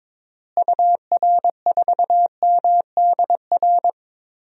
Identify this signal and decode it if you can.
Morse code, heard as a keyed tone, about 22 wpm, 700 Hz. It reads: UR4MDR
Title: DXCC-100 Call Signs - 6 Characters - Repeated 3x - 22wpm